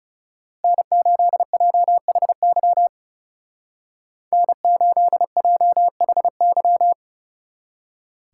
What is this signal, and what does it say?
Morse code, heard as a keyed tone, about 36 wpm, 700 Hz. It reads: N8JHY N8JHY
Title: Speed Racing - USA 1x3 Call Signs 30wpm